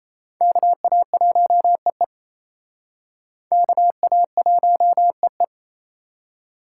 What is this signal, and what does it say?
Morse code, heard as a keyed tone, about 34 wpm, 700 Hz. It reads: KA1EE KA1EE
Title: Speed Racing - USA 2x2 Call Signs 28wpm